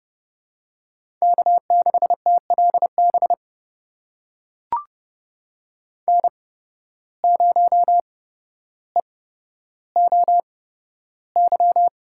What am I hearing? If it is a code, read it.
Morse code, heard as a keyed tone, about 32 wpm, spaced as for 13 wpm, 700 Hz. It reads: K6TLB N0EOY
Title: USA Call Signs - 5 Characters - 8x Character Spacing - 30wpm